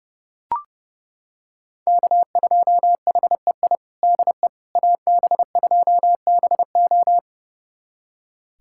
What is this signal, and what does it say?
Morse code, heard as a keyed tone, about 32 wpm, 700 Hz. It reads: K2HEI DE AB2BO
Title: USA Callsign DE Callsign - 5 Characters - 30wpm